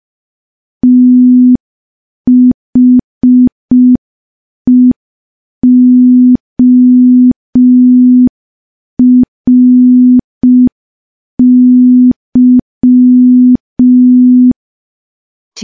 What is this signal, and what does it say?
Morse code, heard as a keyed tone, about 5 wpm, 250 Hz. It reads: THEORY
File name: morse / morse-training-05WPM